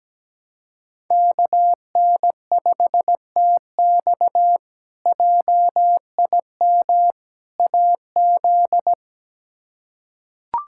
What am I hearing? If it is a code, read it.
Morse code, heard as a keyed tone, about 17 wpm, 700 Hz. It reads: KN5TX JIM AZ
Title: K1USN SST QSO Practice - 17wpm